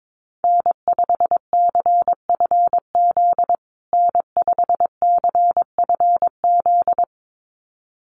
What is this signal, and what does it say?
Morse code, heard as a keyed tone, about 22 wpm, 700 Hz. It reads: N5CFZ N5CFZ
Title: USA Call Signs Repeated 2x 22wpm